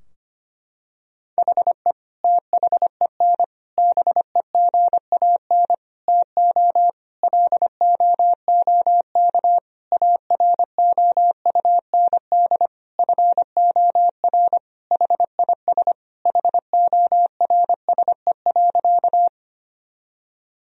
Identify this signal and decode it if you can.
Morse code, heard as a keyed tone, about 25 wpm, 700 Hz. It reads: HE THEN BEGAN TO LOOK AROUND FOR HIS HORSE.
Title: Sentences from Top 1000 Words - Review Part 3 - 25wpm